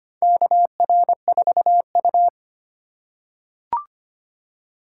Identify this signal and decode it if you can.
Morse code, heard as a keyed tone, about 25 wpm, 700 Hz. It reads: KR4U
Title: USA Call Signs - 4 Characters - 4x Character Spacing - 25wpm